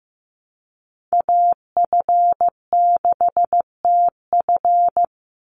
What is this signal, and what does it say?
Morse code, heard as a keyed tone, about 15 wpm, 700 Hz. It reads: AF6TF
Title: USA Call Signs 15wpm